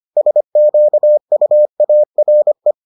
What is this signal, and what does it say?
Morse code, heard as a keyed tone, about 25 wpm, 600 Hz. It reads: SQUARE